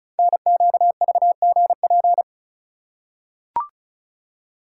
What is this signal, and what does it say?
Morse code, heard as a keyed tone, about 35 wpm, 690 Hz. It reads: NQVGP
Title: Code Groups - 5 Letters - 35wpm